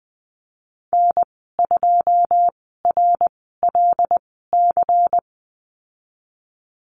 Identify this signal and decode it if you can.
Morse code, heard as a keyed tone, about 20 wpm, spaced as for 15 wpm, 700 Hz. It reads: N2RLC
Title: 186 Lesson 30g - Letter b - Call Signs 20wpm